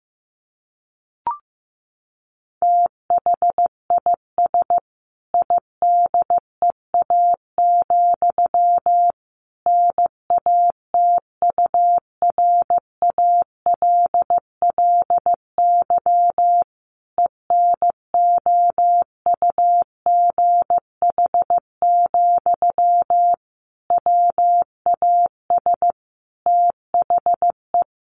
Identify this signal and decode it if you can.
Morse code, heard as a keyed tone, about 15 wpm, 700 Hz. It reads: THIS IDEA, NATURALLY ENOUGH, WAS THE
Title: Sentences from Top 1300 Words 15wpm